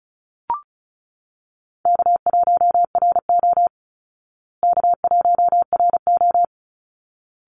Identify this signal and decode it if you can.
Morse code, heard as a keyed tone, about 34 wpm, 700 Hz. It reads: K1RO K1RO
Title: Call Signs Repeated 3x - 35wpm